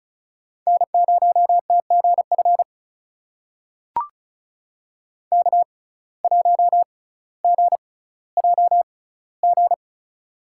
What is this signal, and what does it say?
Morse code, heard as a keyed tone, about 36 wpm, spaced as for 18 wpm, 700 Hz. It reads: N0TGF K1GJG
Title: USA Call Signs - 5 Characters - 6x Character Spacing - 35wpm